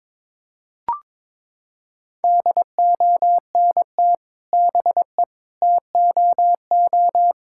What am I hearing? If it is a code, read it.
Morse code, heard as a keyed tone, about 21 wpm, 700 Hz. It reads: DONT BE TOO
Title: Sentences from Top 1000 Words - Review Part 5 - 22wpm